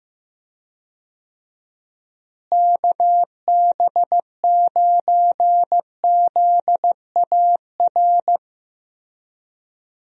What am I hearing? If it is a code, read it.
Morse code, heard as a keyed tone, about 15 wpm, 700 Hz. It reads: KB9ZAR
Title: USA Call Signs - 6 Characters - Repeated 3x - 15wpm